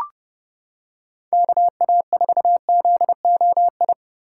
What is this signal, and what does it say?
Morse code, heard as a keyed tone, about 29 wpm, 700 Hz. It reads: KA4ZOI